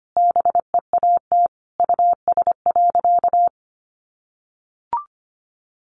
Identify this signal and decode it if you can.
Morse code, heard as a keyed tone, about 25 wpm, 700 Hz. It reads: BEAT US.
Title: Sentences from Top 1300 Words 25wpm